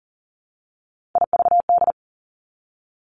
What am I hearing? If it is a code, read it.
Morse code, heard as a keyed tone, about 40 wpm, 700 Hz. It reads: IVD
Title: Three Challenging Letter-Number Combinations 40wpm